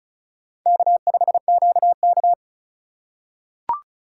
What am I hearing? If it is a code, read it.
Morse code, heard as a keyed tone, about 34 wpm, 700 Hz. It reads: K5QK
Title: USA Call Signs - 3x Character Spacing - 35wpm